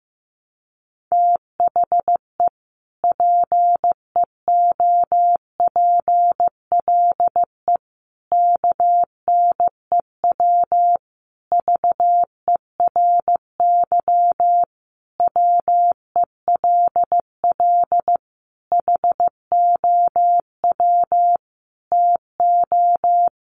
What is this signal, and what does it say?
Morse code, heard as a keyed tone, about 15 wpm, 700 Hz. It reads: THE PEOPLE KNEW VERY WELL HOW TO
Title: Sentences from Top 1100 Words 15wpm